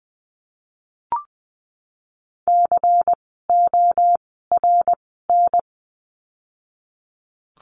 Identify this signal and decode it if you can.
Morse code, heard as a keyed tone, about 20 wpm, spaced as for 15 wpm, 700 Hz. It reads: CORN
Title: 094 Lesson 16f - Letter c - Words 20wpm